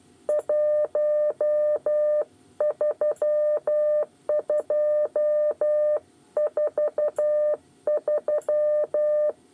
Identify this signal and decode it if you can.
Morse code, heard as a keyed tone, about 12 wpm, 600 Hz. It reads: 13243
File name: 8 bis lezione cw 2